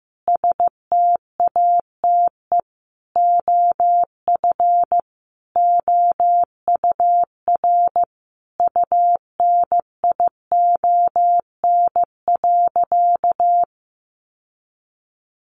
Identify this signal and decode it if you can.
Morse code, heard as a keyed tone, about 15 wpm, 700 Hz. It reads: STATE OF OUR UNION.
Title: Sentences from Top 1400 Words 15wpm